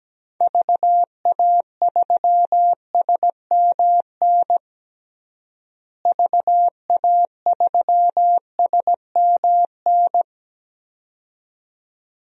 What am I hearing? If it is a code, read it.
Morse code, heard as a keyed tone, about 17 wpm, 700 Hz. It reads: VA3SMN VA3SMN
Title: DXCC-100 Call Signs - 6 Characters - Repeated 2x - 17wpm